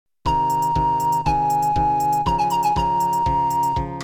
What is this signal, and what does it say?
There is no code.